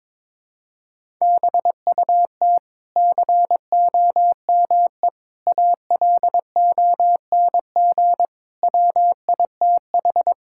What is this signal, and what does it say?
Morse code, heard as a keyed tone, about 22 wpm, 700 Hz. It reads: BUT COME ALONG WITH
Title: Sentences from Top 1000 Words - Review Part 6 - 22wpm